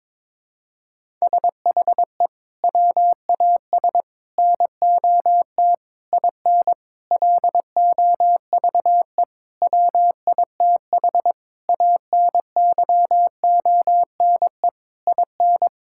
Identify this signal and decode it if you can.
Morse code, heard as a keyed tone, about 22 wpm, 700 Hz. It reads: SHE WAS NOT IN LOVE WITH ANYONE IN
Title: Sentences from Top 1000 Words - Review Part 5 - 22wpm